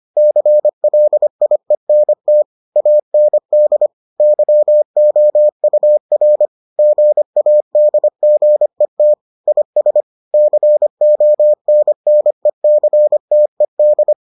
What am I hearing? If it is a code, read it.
Morse code, heard as a keyed tone, about 25 wpm, 600 Hz. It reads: CLIENT AND YOUR GADGET IS CONNECTED